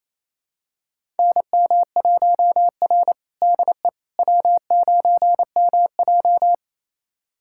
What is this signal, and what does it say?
Morse code, heard as a keyed tone, about 28 wpm, 700 Hz. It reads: NM1R DE W9MJ
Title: USA Callsign DE Callsign - 4 Characters - 28wpm